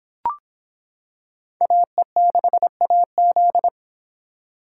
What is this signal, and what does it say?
Morse code, heard as a keyed tone, about 26 wpm, 700 Hz. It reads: AE6AZ